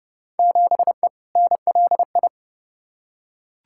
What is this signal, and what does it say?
Morse code, heard as a keyed tone, about 30 wpm, 700 Hz. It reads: 7E NLI
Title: Field Day Exchanges Encore - 30wpm